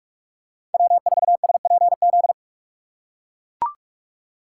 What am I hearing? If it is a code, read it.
Morse code, heard as a keyed tone, about 44 wpm, 690 Hz. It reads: W4SPZ
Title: USA Call Signs - 5 Characters - 6x Character Spacing - 45wpm